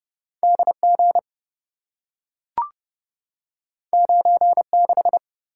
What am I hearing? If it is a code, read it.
Morse code, heard as a keyed tone, about 28 wpm, 700 Hz. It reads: DG 96